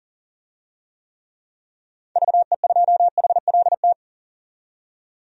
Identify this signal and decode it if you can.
Morse code, heard as a keyed tone, about 38 wpm, 700 Hz. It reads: VE2HLT
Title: DXCC-100 Call Signs - 6 Characters - 40wpm